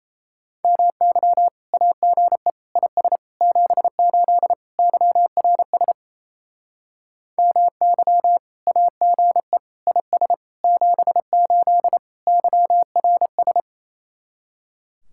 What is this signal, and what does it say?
Morse code, heard as a keyed tone, about 35 wpm, 700 Hz. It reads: MY AGE IS 78 YRS MY AGE IS 78 YRS
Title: Speed-Racing Common QSO Phrases - 28wpm